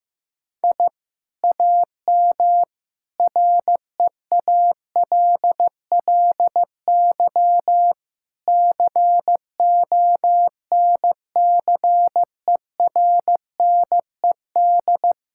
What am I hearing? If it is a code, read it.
Morse code, heard as a keyed tone, about 15 wpm, 700 Hz. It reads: I AM REALLY CONCERNED
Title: Sentences from Top 1000 Words - Review Part 6 - 15wpm